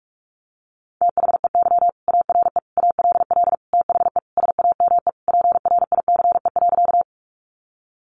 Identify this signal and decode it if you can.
Morse code, heard as a keyed tone, about 44 wpm, 700 Hz. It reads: THEY ARE ALL THE SAME PRICE.
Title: Sentences from Top 600 Words 45wpm